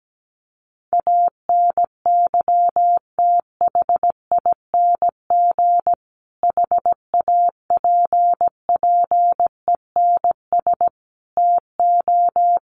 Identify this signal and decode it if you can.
Morse code, heard as a keyed tone, about 17 wpm, 700 Hz. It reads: ANYTHING HAPPENS TO
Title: A Sound Of Thunder 17wpm